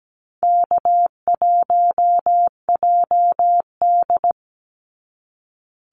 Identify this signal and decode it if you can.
Morse code, heard as a keyed tone, about 17 wpm, 700 Hz. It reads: K1JD
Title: NA SOTA Activator Callsigns 17wpm